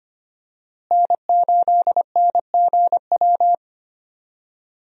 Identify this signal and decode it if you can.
Morse code, heard as a keyed tone, about 25 wpm, 700 Hz. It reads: N8NGW
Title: USA Call Signs - 5 Characters - Repeated 2x - 25wpm